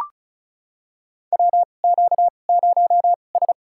Morse code, heard as a keyed tone, about 36 wpm, spaced as for 25 wpm, 700 Hz. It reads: WQ0S
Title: USA Call Signs - 4 Characters - 2x Character Spacing - 35wpm